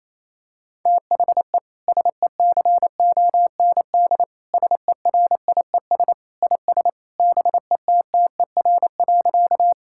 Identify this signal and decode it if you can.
Morse code, heard as a keyed tone, about 28 wpm, 700 Hz. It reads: THE SECOND SERIES IS BETTER.
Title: Sentences from Top 1000 Words - Review Part 5 - 28wpm